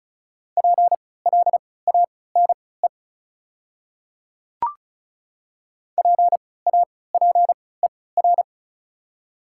Morse code, heard as a keyed tone, about 34 wpm, spaced as for 20 wpm, 700 Hz. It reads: PLANE PAPER
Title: 3x Character Spacing - Top 500 Words - 35wpm